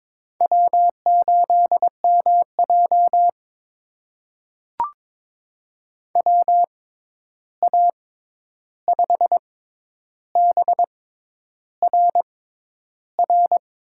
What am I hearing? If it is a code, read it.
Morse code, heard as a keyed tone, about 21 wpm, spaced as for 10 wpm, 700 Hz. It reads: W8MJ WA5BRR
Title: USA Call Signs - 6x Character Spacing - 22wpm